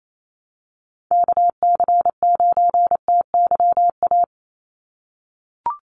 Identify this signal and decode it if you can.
Morse code, heard as a keyed tone, about 27 wpm, 700 Hz. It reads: KC9TYA
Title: NA SOTA Chaser Callsigns 28wpm